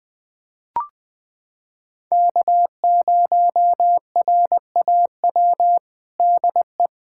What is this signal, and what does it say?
Morse code, heard as a keyed tone, about 20 wpm, 700 Hz. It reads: K0RAW DE
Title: USA Callsign DE Callsign - 5 Characters - 20wpm